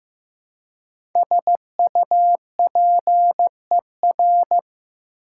Morse code, heard as a keyed tone, about 15 wpm, 700 Hz. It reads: SUPER
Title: Prefixes and Suffixes 15wpm